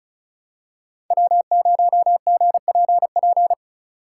Morse code, heard as a keyed tone, about 36 wpm, 700 Hz. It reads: W0GPP
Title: USA Call Signs - 5 Characters - 3x Character Spacing - 35wpm